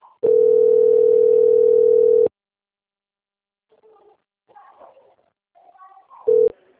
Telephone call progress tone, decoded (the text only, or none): ringback tone